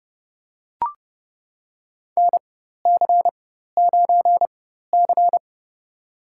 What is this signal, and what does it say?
Morse code, heard as a keyed tone, about 30 wpm, spaced as for 14 wpm, 700 Hz. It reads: NC9C
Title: USA Call Signs - 4x Character Spacing - 30wpm